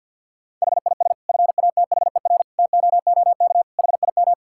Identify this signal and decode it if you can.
Morse code, heard as a keyed tone, about 50 wpm, 700 Hz. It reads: HIS FATHER TOOK HIM
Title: Sentences from Top 1000 Words - Review Part 5 - 50wpm